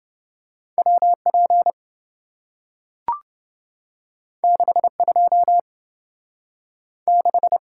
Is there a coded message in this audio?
Yes: Morse code from 0.8 to 7.6 s, heard as a keyed tone, about 27 wpm, 700 Hz. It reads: WP 62 6